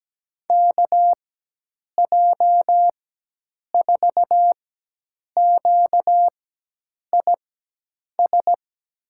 Morse code, heard as a keyed tone, about 17 wpm, spaced as for 8 wpm, 700 Hz. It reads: KJ4QIS
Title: USA Call Signs - 4x Character Spacing - 17wpm